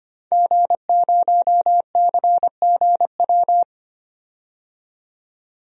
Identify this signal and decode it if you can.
Morse code, heard as a keyed tone, about 25 wpm, 700 Hz. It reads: G0CGW
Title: DXCC-100 Call Signs - 5 Characters - 25wpm